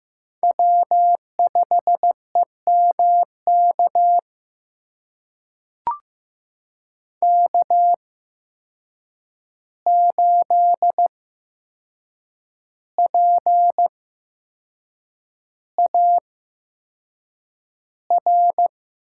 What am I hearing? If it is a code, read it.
Morse code, heard as a keyed tone, about 15 wpm, spaced as for 6 wpm, 700 Hz. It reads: W5EMK K8PAR
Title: USA Call Signs - 5 Characters - 8x Character Spacing - 15wpm